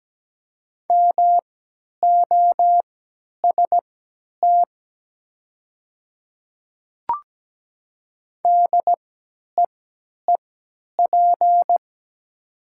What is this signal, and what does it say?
Morse code, heard as a keyed tone, about 17 wpm, spaced as for 10 wpm, 700 Hz. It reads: MOST DEEP
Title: 3x Character Spacing - Top 500 Words - 17wpm